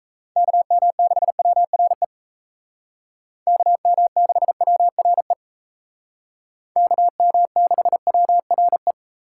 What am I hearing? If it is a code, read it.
Morse code, heard as a keyed tone, about 33 wpm, 700 Hz. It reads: KM6WRE KM6WRE KM6WRE
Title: Speed Racing - USA 2x3 Call Signs 28wpm